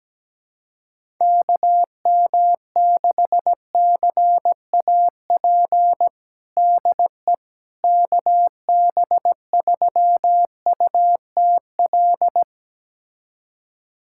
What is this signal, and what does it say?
Morse code, heard as a keyed tone, about 17 wpm, 700 Hz. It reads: KM6CAP DE KB3UTL
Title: USA Callsign DE Callsign 17wpm